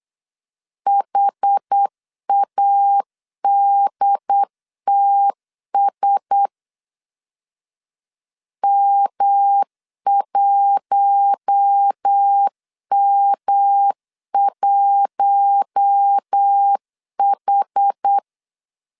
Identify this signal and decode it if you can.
Morse code, heard as a keyed tone, about 8 wpm, 800 Hz. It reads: HADTS M1M1H
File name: CW / DET / 1 6 1500C S B 8wpm 5142